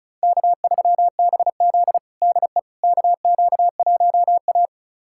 Morse code, heard as a keyed tone, about 36 wpm, 700 Hz. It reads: K3BZ DE KQ1A
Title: USA Callsign DE Callsign - 4 Characters - 35wpm